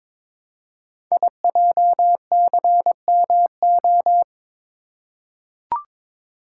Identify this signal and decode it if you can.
Morse code, heard as a keyed tone, about 22 wpm, 700 Hz. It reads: IJCMO
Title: Code Groups - 5 Letters - 22wpm